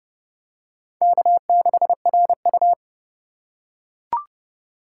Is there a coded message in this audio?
Yes: Morse code from 1.0 to 2.7 s, heard as a keyed tone, about 29 wpm, 700 Hz. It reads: K6RU